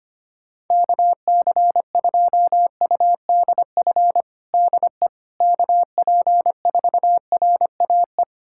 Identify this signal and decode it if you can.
Morse code, heard as a keyed tone, about 25 wpm, 700 Hz. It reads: KC2UDF DE KP4RAE
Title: USA Callsign DE Callsign - 6 Characters - 25wpm